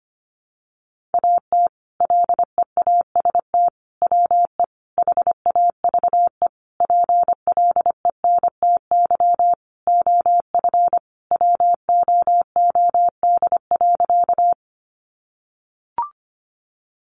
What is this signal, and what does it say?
Morse code, heard as a keyed tone, about 25 wpm, 700 Hz. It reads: AT LEAST WE HAVE PLENTY OF WOOD.
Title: Sentences from Top 1300 Words 25wpm